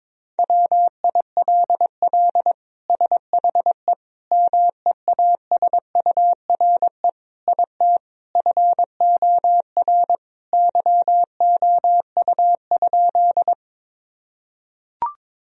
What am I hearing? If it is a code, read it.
Morse code, heard as a keyed tone, about 22 wpm, 700 Hz. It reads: WILL SHE MEASURE IT FOR YOU?
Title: Sentences from Top 1000 Words - Review Part 2 - 22wpm